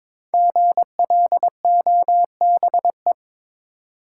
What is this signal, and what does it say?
Morse code, heard as a keyed tone, about 22 wpm, 700 Hz. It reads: GLOBE
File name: 5 Letter Words - First 3 Letters Given - 22wpm